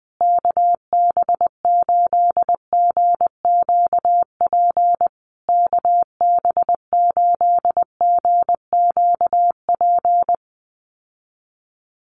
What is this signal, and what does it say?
Morse code, heard as a keyed tone, about 20 wpm, 700 Hz. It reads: KB8GQP KB8GQP
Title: USA Call Signs Repeated 3x 20wpm